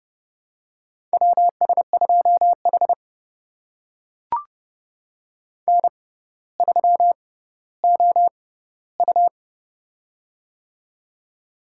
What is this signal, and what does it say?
Morse code, heard as a keyed tone, about 31 wpm, spaced as for 15 wpm, 700 Hz. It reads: WS2H N3OU
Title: USA Call Signs - 6x Character Spacing - 30wpm